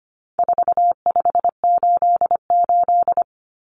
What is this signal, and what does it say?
Morse code, heard as a keyed tone, about 25 wpm, 700 Hz. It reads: 4588
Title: Common Cut Numbers - 4-Digit 25wpm